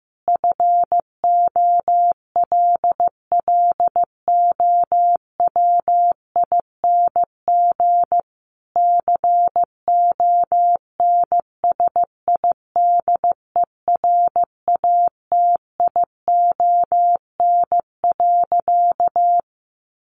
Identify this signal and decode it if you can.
Morse code, heard as a keyed tone, about 15 wpm, 700 Hz. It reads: FOLLOWING CONSIDERATION.
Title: Sentences from Top 1500 Words 15wpm